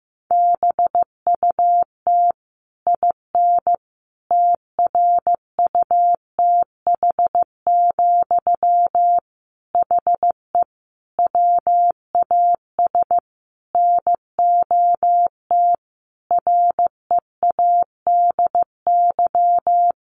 Morse code, heard as a keyed tone, about 15 wpm, 700 Hz. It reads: BUT IN TRUTH, HE WAS NOT READY
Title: Sentences from Top 1500 Words 15wpm